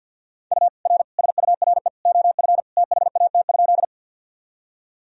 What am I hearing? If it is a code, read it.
Morse code, heard as a keyed tone, about 48 wpm, 700 Hz. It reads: U R SURE OF THAT?
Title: Sentences from Top 300 Words and Abbreviations 50wpm